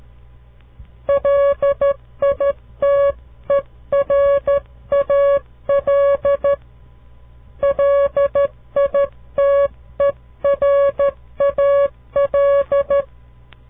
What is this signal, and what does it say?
Morse code, heard as a keyed tone, about 13 wpm, 560 Hz. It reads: LITERAL LITERAL
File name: CW Learning Programs / farnsworth / unit 2